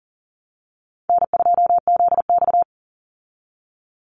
Morse code, heard as a keyed tone, about 40 wpm, 700 Hz. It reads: N2ZX
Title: Call Signs Repeated 3x - 40wpm